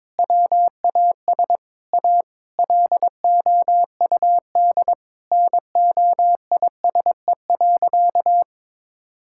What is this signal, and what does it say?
Morse code, heard as a keyed tone, about 22 wpm, 700 Hz. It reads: WAS A LOUD NOISE.
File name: practice / 22wpm / Sentences with 3-Word Phrases - 22wpm